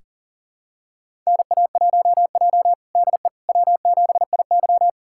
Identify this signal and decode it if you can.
Morse code, heard as a keyed tone, about 37 wpm, 700 Hz. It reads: NA1J DE W7IY
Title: USA Callsign DE Callsign - 4 Characters - 40wpm